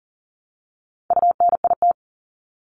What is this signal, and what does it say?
Morse code, heard as a keyed tone, about 38 wpm, 700 Hz. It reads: UNIT
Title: Top 500 Words Repeated 40wpm